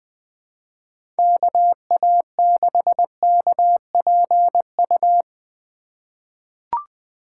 Morse code, heard as a keyed tone, about 20 wpm, 700 Hz. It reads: KA6KPU
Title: USA Call Signs - 2x Character Spacing - 20wpm